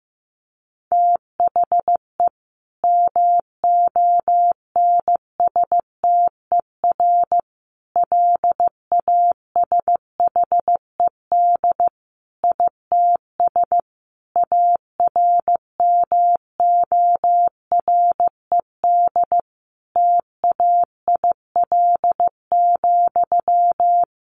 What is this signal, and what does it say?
Morse code, heard as a keyed tone, about 14 wpm, 700 Hz. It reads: THE MONSTER LASHED ITS ARMORED TAIL,
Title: A Sound Of Thunder 15wpm